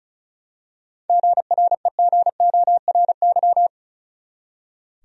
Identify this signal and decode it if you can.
Morse code, heard as a keyed tone, about 36 wpm, 700 Hz. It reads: GREGORY
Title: USA Common Names - 35wpm